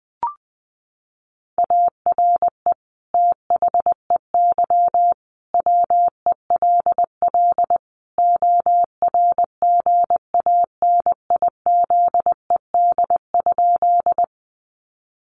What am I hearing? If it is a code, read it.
Morse code, heard as a keyed tone, about 20 wpm, 700 Hz. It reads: ARE THEY WELL ORGANIZED?
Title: Sentences from Top 1300 Words 20wpm